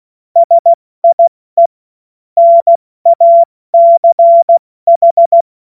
Morse code, heard as a keyed tone, about 16 wpm, 680 Hz. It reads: SIE NACH